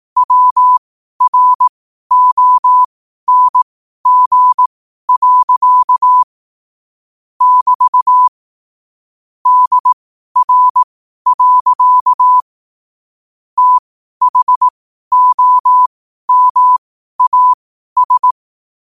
Morse code, heard as a keyed tone, about 18 wpm, spaced as for 12 wpm, 1000 Hz. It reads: WRONG. <BT> DR. THOMAS
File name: qotdfiles / 12wpm